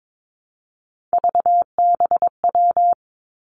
Morse code, heard as a keyed tone, about 22 wpm, 700 Hz. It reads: VBW
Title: Three Challenging Letter-Number Combinations 22wpm